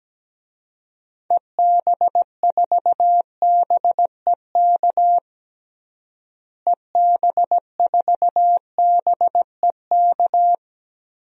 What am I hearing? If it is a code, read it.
Morse code, heard as a keyed tone, about 17 wpm, 700 Hz. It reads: EB4BEK EB4BEK
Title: DXCC-100 Call Signs - 6 Characters - Repeated 3x - 17wpm